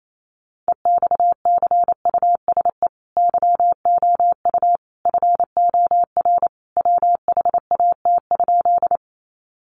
Morse code, heard as a keyed tone, about 28 wpm, 700 Hz. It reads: EXCUSE YOU FOR WHAT?
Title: Sentences from Top 700 Words 28wpm